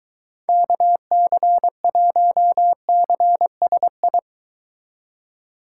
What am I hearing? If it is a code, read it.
Morse code, heard as a keyed tone, about 24 wpm, 700 Hz. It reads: KC1CSI